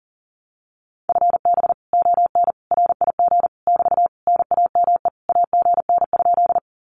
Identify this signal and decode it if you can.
Morse code, heard as a keyed tone, about 40 wpm, 700 Hz. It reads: FB ON RIG <BT> NAME AGN?
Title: Common QSO Phrases Encore 40wpm